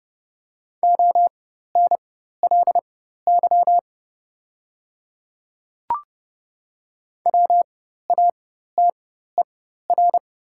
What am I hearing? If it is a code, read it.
Morse code, heard as a keyed tone, about 30 wpm, spaced as for 14 wpm, 700 Hz. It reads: ONLY WATER
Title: 4x Character Spacing - Top 200 Words - 30wpm